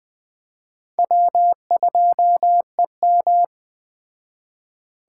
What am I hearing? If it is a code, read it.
Morse code, heard as a keyed tone, about 21 wpm, 700 Hz. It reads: W2EM
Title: Speed Racing - USA 1x2 Call Signs 17wpm